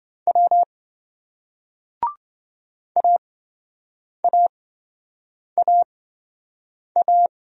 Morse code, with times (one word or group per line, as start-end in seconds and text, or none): 0.3-0.6 W
3.0-7.3 AAAA